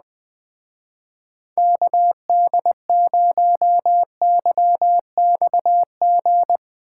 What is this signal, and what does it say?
Morse code, heard as a keyed tone, about 21 wpm, 700 Hz. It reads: KD0YXG